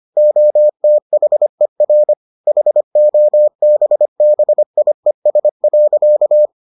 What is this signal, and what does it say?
Morse code, heard as a keyed tone, about 25 wpm, 600 Hz. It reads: OTHER HOBBIES.